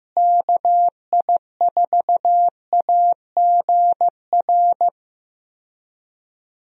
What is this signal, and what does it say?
Morse code, heard as a keyed tone, about 15 wpm, 700 Hz. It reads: KI4AGR
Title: USA Call Signs - 6 Characters - Repeated 2x - 15wpm